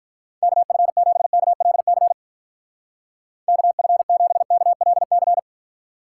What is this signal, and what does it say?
Morse code, heard as a keyed tone, about 46 wpm, 700 Hz. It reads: KF7KLC KF7KLC